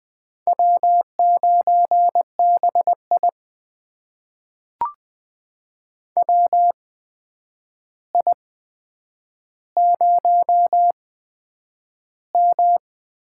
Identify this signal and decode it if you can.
Morse code, heard as a keyed tone, about 20 wpm, spaced as for 9 wpm, 700 Hz. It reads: W9BI WI0M
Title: USA Call Signs - 4 Characters - 8x Character Spacing - 20wpm